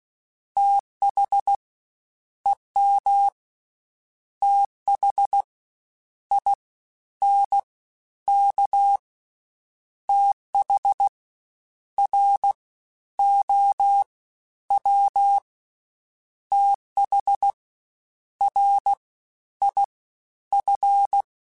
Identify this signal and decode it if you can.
Morse code, heard as a keyed tone, about 16 wpm, 800 Hz. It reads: TH EM TH I N K TH R O W TH R I F
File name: Misc / Misc-mp3s / Jumps @15